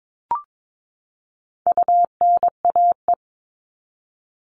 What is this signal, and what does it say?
Morse code, heard as a keyed tone, about 22 wpm, 700 Hz. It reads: UNAE
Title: All Cut Numbers - 4-Digit 22wpm